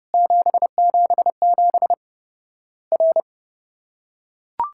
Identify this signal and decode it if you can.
Morse code, heard as a keyed tone, about 30 wpm, 700 Hz. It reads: 777 R
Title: Worked All Europe DX Contest - QTC Exchange - 30wpm